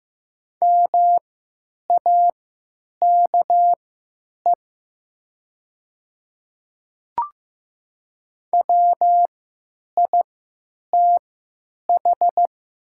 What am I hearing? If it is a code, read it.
Morse code, heard as a keyed tone, about 15 wpm, spaced as for 9 wpm, 700 Hz. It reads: MAKE WITH
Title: 3x Character Spacing - Top 100 Words - 15wpm